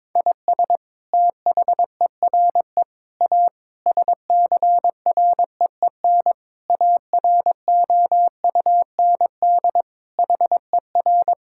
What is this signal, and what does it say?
Morse code, heard as a keyed tone, about 22 wpm, 700 Hz. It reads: IS THERE A SCREEN AROUND HER
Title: Sentences from Top 1000 Words - Review Part 6 - 22wpm